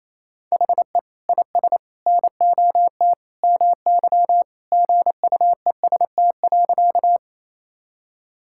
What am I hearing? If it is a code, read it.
Morse code, heard as a keyed tone, about 28 wpm, 700 Hz. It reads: HE IS NOT MY GUEST.
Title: Sentences from Top 1600 Words 28wpm